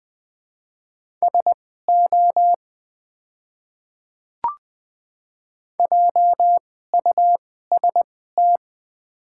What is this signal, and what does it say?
Morse code, heard as a keyed tone, about 20 wpm, spaced as for 14 wpm, 700 Hz. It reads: SO JUST
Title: 2x Character Spacing - Top 500 Words - 20wpm